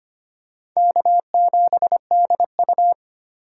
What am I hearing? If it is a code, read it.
Morse code, heard as a keyed tone, about 25 wpm, 700 Hz. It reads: K7DU
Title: USA Call Signs - 4 Characters - 4x Character Spacing - 25wpm